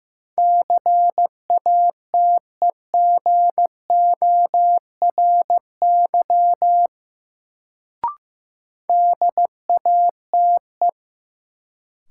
Morse code, heard as a keyed tone, about 15 wpm, 700 Hz. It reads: CATEGORY DATE
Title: Sentences from Top 1000 Words - Review Part 7 - New Word Drill - 15wpm